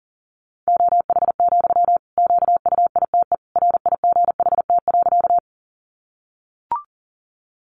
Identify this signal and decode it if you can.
Morse code, heard as a keyed tone, about 40 wpm, 700 Hz. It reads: OH, QUITE RIGHT.
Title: Sentences from Top 700 Words 40wpm